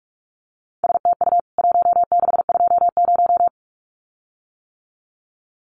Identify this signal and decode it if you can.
Morse code, heard as a keyed tone, about 44 wpm, 700 Hz. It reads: STU 1620
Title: CWT Exchange 45wpm